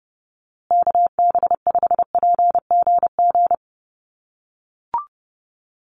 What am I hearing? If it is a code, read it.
Morse code, heard as a keyed tone, about 30 wpm, 700 Hz. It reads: KB5PGG
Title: USA Call Signs Repeated 2x 30wpm